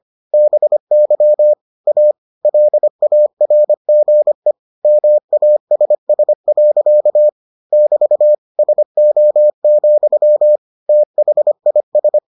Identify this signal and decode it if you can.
Morse code, heard as a keyed tone, about 25 wpm, 600 Hz. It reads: BY A LARGE MASS. <BT> SO, THIS